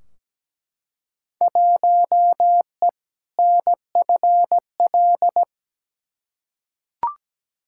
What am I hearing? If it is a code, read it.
Morse code, heard as a keyed tone, about 17 wpm, 700 Hz. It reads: 1E NFL
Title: Field Day Exchanges - 17wpm